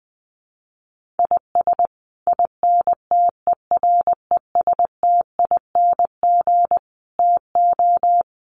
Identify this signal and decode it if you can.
Morse code, heard as a keyed tone, about 20 wpm, 700 Hz. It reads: IS INTERESTING TO
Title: Sets of 5 Words Encore2 20wpm